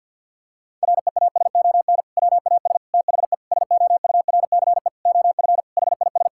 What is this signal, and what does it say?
Morse code, heard as a keyed tone, about 50 wpm, 700 Hz. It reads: REASON WAS THE SOURCE OF HIS
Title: Sentences from Top 1000 Words - Review Part 5 - 50wpm